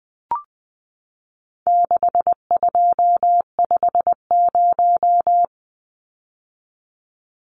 Morse code, heard as a keyed tone, about 20 wpm, 700 Hz. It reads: 6250
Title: Numbers - 4-Digit 20wpm